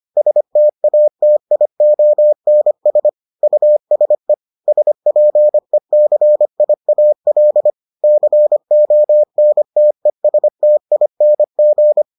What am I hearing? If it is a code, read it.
Morse code, heard as a keyed tone, about 25 wpm, 600 Hz. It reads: STATIONS USE SPECIAL CONTESTING